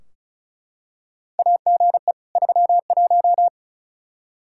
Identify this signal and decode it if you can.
Morse code, heard as a keyed tone, about 36 wpm, 700 Hz. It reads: AGE 31
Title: Practice QSO - Exchange 3 - More Info - 35wpm